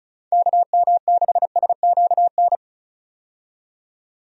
Morse code, heard as a keyed tone, about 36 wpm, 700 Hz. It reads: KM6SQN
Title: USA Call Signs - 6 Characters - 35wpm